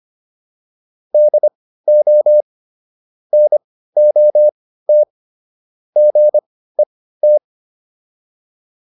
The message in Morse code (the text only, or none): DO NOT GET